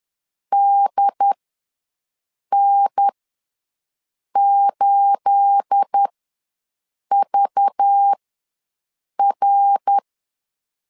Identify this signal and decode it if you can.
Morse code, heard as a keyed tone, about 10 wpm, spaced as for 6 wpm, 800 Hz. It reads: DN8VR
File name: LEZIONI / 10 SIGNS / 10A SLASH ws 10wpm 4121